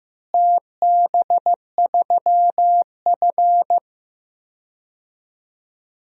Morse code, heard as a keyed tone, about 15 wpm, 700 Hz. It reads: TB3F
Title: DXCC-100 Call Signs - 4 Characters - Repeated 2x - 15wpm